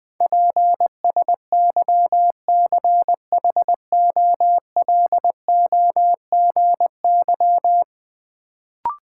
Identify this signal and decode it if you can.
Morse code, heard as a keyed tone, about 21 wpm, 700 Hz. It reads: PSYCHOLOGY
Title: Sentences from Top 1400 Words - New Word Drill Alt - 20wpm